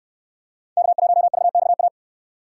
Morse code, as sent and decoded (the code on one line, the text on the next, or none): -.- .---- ...- -..- .-
K1VXA